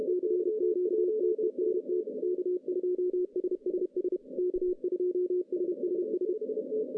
Morse code, heard as a keyed tone, about 31 wpm, 380 Hz. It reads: CQTEST K2SSS K2SSS
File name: 2015 cq ww cw / 2015 cqwww k5tr - 0-41